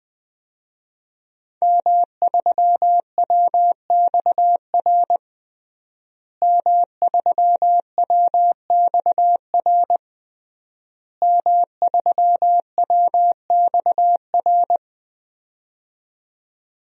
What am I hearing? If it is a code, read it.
Morse code, heard as a keyed tone, about 20 wpm, 700 Hz. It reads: M3WXR M3WXR M3WXR
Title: DXCC-100 Call Signs - 5 Characters - Repeated 3x - 20wpm